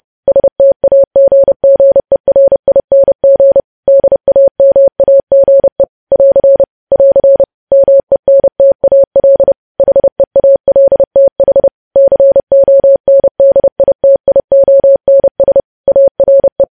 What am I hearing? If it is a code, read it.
Morse code, heard as a keyed tone, about 30 wpm, 550 Hz. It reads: STAGGERING DAMAGE <AR> <AR> MENTAL HEALTH CONDITIONS ARE
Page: News Headlines in Morse Code at 30 WPM